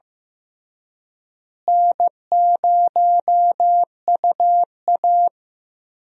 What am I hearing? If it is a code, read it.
Morse code, heard as a keyed tone, about 14 wpm, 700 Hz. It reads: N0UA
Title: USA Call Signs - 4 Characters - Repeated 3x - 15wpm